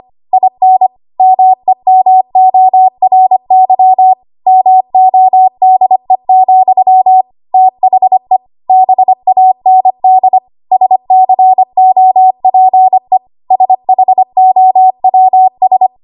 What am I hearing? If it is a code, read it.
Morse code, heard as a keyed tone, about 25 wpm, 750 Hz. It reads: IN MEMORY MODE, THE BAND SCOPE SHOWS
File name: Archive / 25 WPM / 25WPM